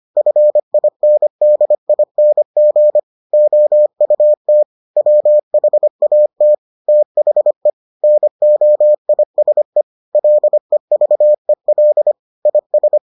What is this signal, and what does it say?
Morse code, heard as a keyed tone, about 25 wpm, 600 Hz. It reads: FINDING OUT WHAT THE NOISE LEVEL IS